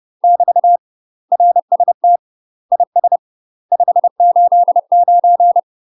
Morse code, heard as a keyed tone, about 30 wpm, 700 Hz. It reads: <BT> RST IS 589
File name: UK MASTER QSOs TRAINING / 30WPM / UK MASTER QSOs 0050